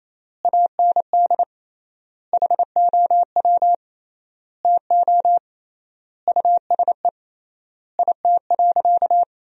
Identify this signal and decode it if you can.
Morse code, heard as a keyed tone, about 28 wpm, 700 Hz. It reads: AND HOW TO USE IT.
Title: 3x Word-Spacing - Sentences from Top 500 Words - 28wpm